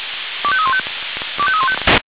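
Five-tone selective call (ZVEI), heard as quick runs of five tones, tones of about 70 ms. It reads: 25315 25315
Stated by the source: ZVEI five-tone selective call received by SDR.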